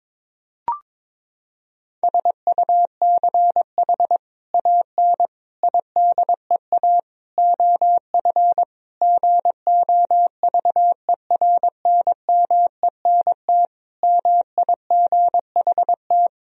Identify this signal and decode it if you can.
Morse code, heard as a keyed tone, about 22 wpm, 700 Hz. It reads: SUCH AN IDEA OF GOVERNMENT MIGHT
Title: Sentences from Top 1000 Words - Review Part 3 - 22wpm